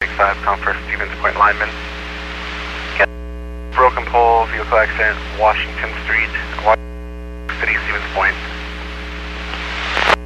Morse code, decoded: none

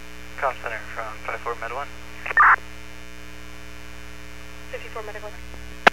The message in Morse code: none

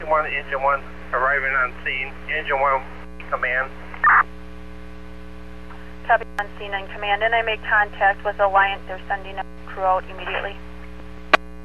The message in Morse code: none

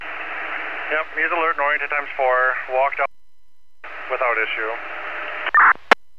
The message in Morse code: none

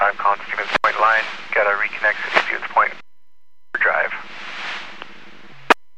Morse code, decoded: none

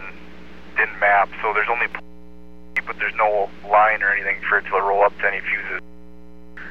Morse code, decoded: none

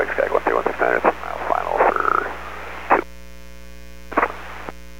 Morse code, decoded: none

